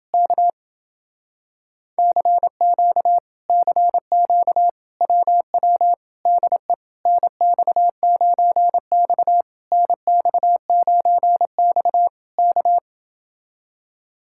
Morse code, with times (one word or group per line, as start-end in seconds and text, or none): 0.1-0.5 K
2.0-3.2 CQ
3.5-4.7 CQ
5.0-5.9 WW
6.3-6.7 DE
7.1-9.4 NX9X
9.7-12.1 NX9X
12.4-12.8 K